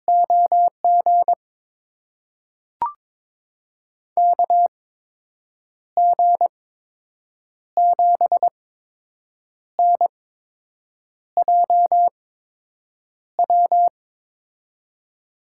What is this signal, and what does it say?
Morse code, heard as a keyed tone, about 22 wpm, spaced as for 7 wpm, 700 Hz. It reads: OG KG7NJW
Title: USA Call Signs - 6 Characters - 8x Character Spacing - 22wpm